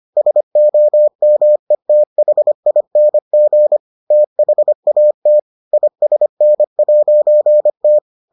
Morse code, heard as a keyed tone, about 25 wpm, 600 Hz. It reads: SOMETHING THAT ISN'T